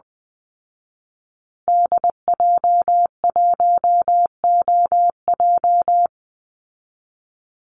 Morse code, heard as a keyed tone, about 20 wpm, 700 Hz. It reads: DJ1OJ
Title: Call Signs Repeated 2x - 20wpm